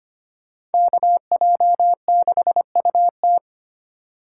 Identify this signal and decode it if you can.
Morse code, heard as a keyed tone, about 32 wpm, 700 Hz. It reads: KJ6UT